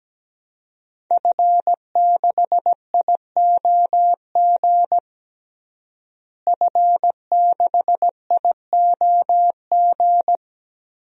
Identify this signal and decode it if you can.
Morse code, heard as a keyed tone, about 17 wpm, 700 Hz. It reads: F6IOG F6IOG
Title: DXCC-100 Call Signs - 5 Characters - Repeated 2x - 17wpm